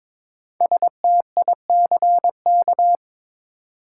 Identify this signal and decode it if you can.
Morse code, heard as a keyed tone, about 22 wpm, 700 Hz. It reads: STICK
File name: 5 Letter Words - First 3 Letters Given - 22wpm